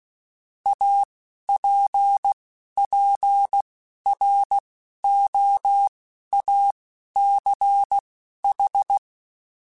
Morse code, heard as a keyed tone, about 16 wpm, spaced as for 11 wpm, 800 Hz. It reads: APPROACH
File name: Technical Subjects / Technical Subjects-mp3s / Gates Intro @10